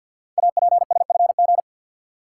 Morse code, heard as a keyed tone, about 48 wpm, 700 Hz. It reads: APSFG
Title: Code Groups - 5 Letters - 50wpm